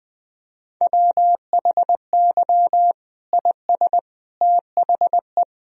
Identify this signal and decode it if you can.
Morse code, heard as a keyed tone, about 20 wpm, 700 Hz. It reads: WHY IS THE
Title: Sentences from Top 1000 Words - Review Part 2 - 20wpm